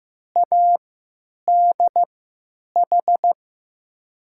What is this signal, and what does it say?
Morse code, heard as a keyed tone, about 15 wpm, spaced as for 9 wpm, 700 Hz. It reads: ADH
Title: USA Call Signs - 3x Character Spacing - 15wpm